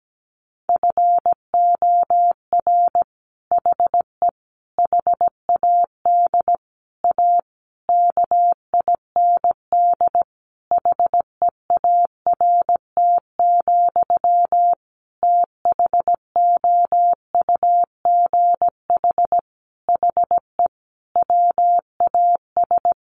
Morse code, heard as a keyed tone, about 17 wpm, 700 Hz. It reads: FOR HE HAD A KIND HEART, THOUGH HE WAS
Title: Sentences from Top 1400 Words 17wpm